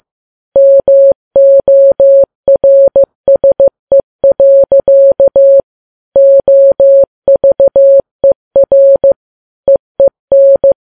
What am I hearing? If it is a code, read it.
Morse code, heard as a keyed tone, about 15 wpm, 550 Hz. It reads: MORSE. OVER EEN
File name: Files Web tls / Morse on9fv / morse15kps